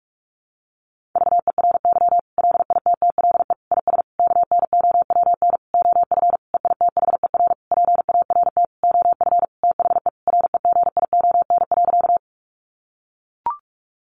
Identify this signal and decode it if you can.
Morse code, heard as a keyed tone, about 44 wpm, 700 Hz. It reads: VERY LITTLE IS KNOWN OF EITHER PART OF THE REGION.
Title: Sentences from Top 1500 Words 45wpm